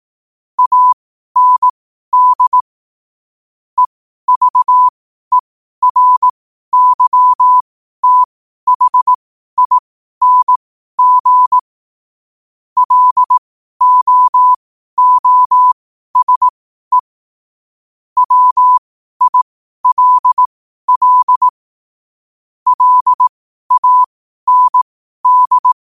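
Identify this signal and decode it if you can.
Morse code, heard as a keyed tone, about 18 wpm, spaced as for 12 wpm, 1000 Hz. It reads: AND EVERYTHING LOOSE WILL LAND
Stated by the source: Morse at 12 wpm.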